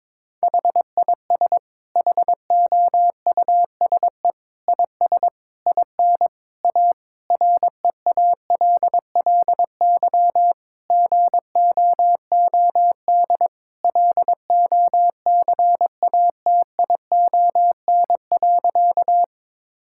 Morse code, heard as a keyed tone, about 22 wpm, 700 Hz. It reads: HIS HOUSE IS IN A REALLY GOOD LOCATION.
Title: Sentences from Top 1000 Words - Review Part 6 - 22wpm